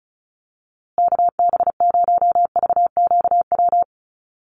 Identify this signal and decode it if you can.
Morse code, heard as a keyed tone, about 34 wpm, 700 Hz. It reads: KB0VQW
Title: USA Call Signs Repeated 2x 35wpm